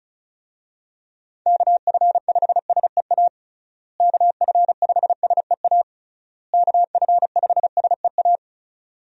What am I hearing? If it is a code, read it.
Morse code, heard as a keyed tone, about 36 wpm, 700 Hz. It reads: KF5SEA KF5SEA KF5SEA
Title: USA Call Signs - 6 Characters - Repeated 3x - 35wpm